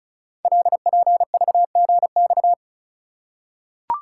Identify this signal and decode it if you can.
Morse code, heard as a keyed tone, about 36 wpm, 700 Hz. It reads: LPVGX
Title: Code Groups - 5 Letters - 35wpm